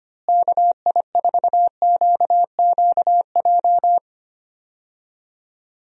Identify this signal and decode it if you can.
Morse code, heard as a keyed tone, about 25 wpm, 700 Hz. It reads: KI4QQJ
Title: USA Call Signs - 6 Characters - Repeated 3x - 25wpm